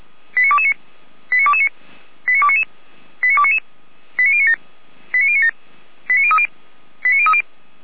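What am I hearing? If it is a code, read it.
Five-tone selective call (ZVEI), heard as quick runs of five tones, tones of about 70 ms. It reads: 89229 89229 89299 89299 89097 89097 89030 89030
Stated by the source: ZVEI five-tone selective call 89277 received on an SDR, logged by the Orlamünde fire brigade (feuerwehr-orlamuende).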